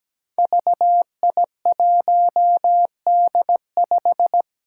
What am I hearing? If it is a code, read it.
Morse code, heard as a keyed tone, about 17 wpm, 700 Hz. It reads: VI1D5
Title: Five Challenging Letter-Number Combinations 17wpm